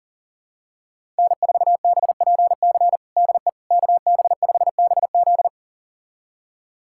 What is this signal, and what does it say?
Morse code, heard as a keyed tone, about 40 wpm, 700 Hz. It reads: N4BPC DE KB5BZ
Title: USA Callsign DE Callsign - 5 Characters - 40wpm